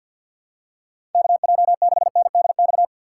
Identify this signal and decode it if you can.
Morse code, heard as a keyed tone, about 48 wpm, 700 Hz. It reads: KJ6NDX
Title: USA Call Signs - 6 Characters - 4x Character Spacing - 50wpm